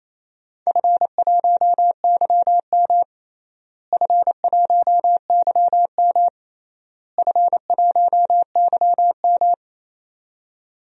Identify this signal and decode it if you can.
Morse code, heard as a keyed tone, about 28 wpm, 700 Hz. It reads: F1YM F1YM F1YM
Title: DXCC-100 Call Signs - 4 Characters - Repeated 3x - 28wpm